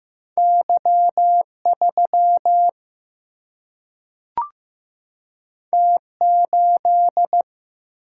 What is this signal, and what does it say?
Morse code, heard as a keyed tone, about 15 wpm, 700 Hz. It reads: Y 3 T 8
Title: Single-Letter Single-Number 15wpm